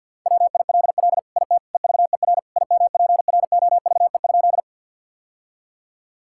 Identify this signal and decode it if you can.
Morse code, heard as a keyed tone, about 52 wpm, 700 Hz. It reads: WILL IT EVER IMPROVE?
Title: Sentences from Top 1000 Words - Review Part 5 - 50wpm